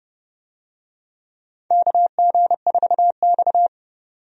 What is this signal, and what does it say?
Morse code, heard as a keyed tone, about 30 wpm, 700 Hz. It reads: KG4X
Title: USA Call Signs - 4 Characters - 30wpm